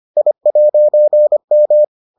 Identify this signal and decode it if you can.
Morse code, heard as a keyed tone, about 25 wpm, 600 Hz. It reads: I'M